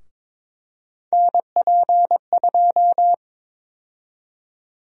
Morse code, heard as a keyed tone, about 22 wpm, 700 Hz. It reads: NP2
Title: USA Call Sign Prefixes - Repeated 2x - 22wpm